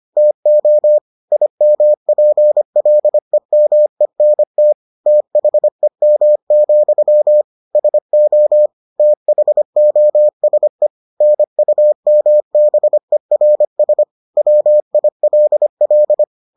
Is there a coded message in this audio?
Yes: Morse code from 0.2 to 16.2 s, heard as a keyed tone, about 25 wpm, 600 Hz. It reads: TO IMPLEMENT THEM, SO THOSE NUMBERS WILL